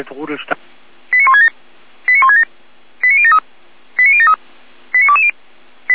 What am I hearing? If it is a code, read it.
Five-tone selective call (ZVEI), heard as quick runs of five tones, tones of about 70 ms, with speech. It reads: 89167 89167 89073 89073 89220